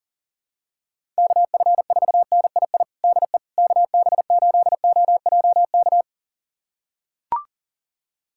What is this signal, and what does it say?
Morse code, heard as a keyed tone, about 37 wpm, 700 Hz. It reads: KF4NII DE KB8OJK
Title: USA Callsign DE Callsign - 6 Characters - 40wpm